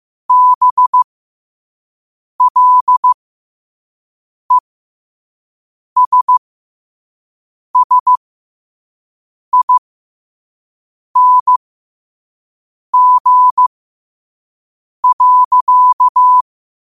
Morse code, heard as a keyed tone, about 15 wpm, spaced as for 5 wpm, 1000 Hz. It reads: BLESSING.
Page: New quotes every day in morse code at 5 Words per minute.